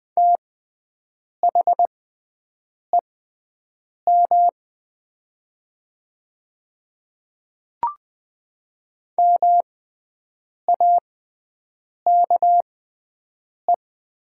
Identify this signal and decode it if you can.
Morse code, heard as a keyed tone, about 20 wpm, spaced as for 7 wpm, 700 Hz. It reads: THEM MAKE
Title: 6x Character Spacing - Top 100 Words - 20wpm